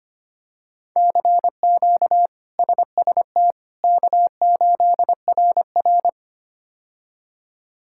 Morse code, heard as a keyed tone, about 25 wpm, 700 Hz. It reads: CQ SST K8RR
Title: K1USN SST QSO Practice - 25wpm